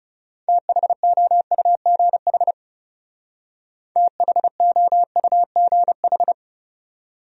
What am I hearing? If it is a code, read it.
Morse code, heard as a keyed tone, about 39 wpm, 700 Hz. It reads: THOUGH THOUGH